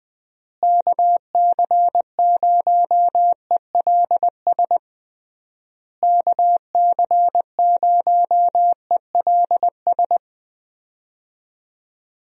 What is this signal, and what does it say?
Morse code, heard as a keyed tone, about 20 wpm, 700 Hz. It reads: KC0ELS KC0ELS
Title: USA Call Signs - 6 Characters - Repeated 3x - 20wpm